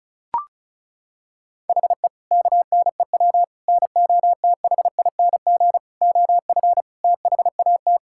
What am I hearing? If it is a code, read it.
Morse code, heard as a keyed tone, about 34 wpm, 700 Hz. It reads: HE KNEW NOTHING OF THAT
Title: Sentences from Top 1600 Words 35wpm